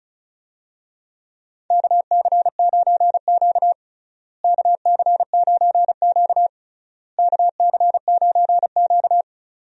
Morse code, heard as a keyed tone, about 34 wpm, 700 Hz. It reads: KC9Q KC9Q KC9Q
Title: USA Call Signs - 4 Characters - Repeated 3x - 35wpm